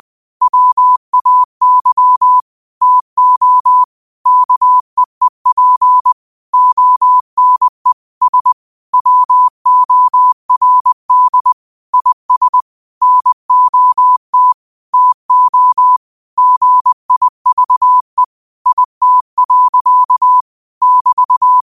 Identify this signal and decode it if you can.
Morse code, heard as a keyed tone, about 20 wpm, 1000 Hz. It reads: WAY TO KEEP ONE S WORD IS NOT TO GIVE IT. <BT>